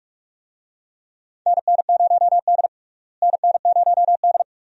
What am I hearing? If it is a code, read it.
Morse code, heard as a keyed tone, about 46 wpm, 700 Hz. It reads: NN0D NN0D
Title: USA Call Signs - 4 Characters - Repeated 2x - 45wpm